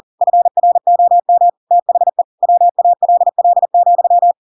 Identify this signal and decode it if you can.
Morse code, heard as a keyed tone, about 40 wpm, 700 Hz. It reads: FROM THE WALL,
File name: Pinocchio / Pinocchio ch4 40 40 ik1hge